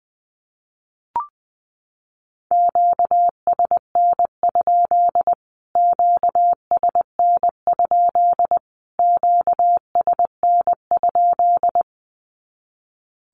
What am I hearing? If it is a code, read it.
Morse code, heard as a keyed tone, about 20 wpm, 700 Hz. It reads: QSN? QSN? QSN?
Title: Q-Codes Repeated 3x 20wpm